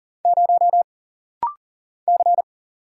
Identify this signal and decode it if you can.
Morse code, heard as a keyed tone, about 40 wpm, 700 Hz. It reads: 0 C
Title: Single Letter-Number Alt - Rapid-Fire 40wpm